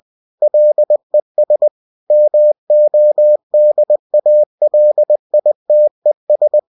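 Morse code, heard as a keyed tone, about 20 wpm, 600 Hz. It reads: LES MODALITES
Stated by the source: Bulletin sent in CW.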